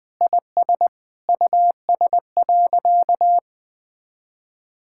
Morse code, heard as a keyed tone, about 21 wpm, 700 Hz. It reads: IS US.
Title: Sentences with 9-Word Phrases - 20wpm